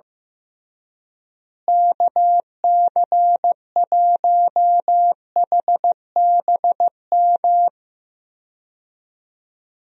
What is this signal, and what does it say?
Morse code, heard as a keyed tone, about 16 wpm, 700 Hz. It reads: KC1HBM
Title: USA Call Signs - 6 Characters - Repeated 3x - 15wpm